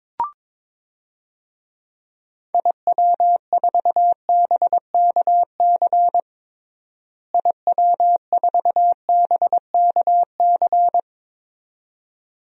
Morse code, heard as a keyed tone, about 22 wpm, 700 Hz. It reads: IW4BKC IW4BKC
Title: DXCC-100 Call Signs - 6 Characters - Repeated 2x - 22wpm